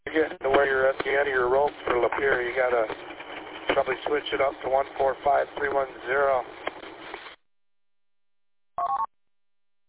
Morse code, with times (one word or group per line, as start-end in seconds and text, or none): none